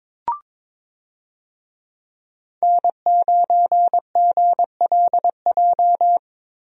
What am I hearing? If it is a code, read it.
Morse code, heard as a keyed tone, about 22 wpm, 700 Hz. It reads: N9GLJ
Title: USA Call Signs - 5 Characters - Repeated 3x - 22wpm